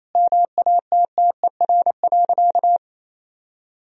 Morse code, heard as a keyed tone, about 27 wpm, 700 Hz. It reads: MATTER.
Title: Sentences from Top 1000 Words - Review Part 3 - 28wpm